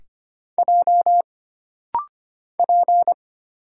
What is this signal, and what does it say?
Morse code, heard as a keyed tone, about 25 wpm, 700 Hz. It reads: J P
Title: Single Letter-Number Alt - Rapid-Fire 25wpm